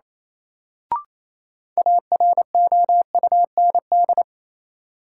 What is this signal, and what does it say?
Morse code, heard as a keyed tone, about 28 wpm, 700 Hz. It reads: AROUND
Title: Sets of 2 Words - New Word Drill - 28wpm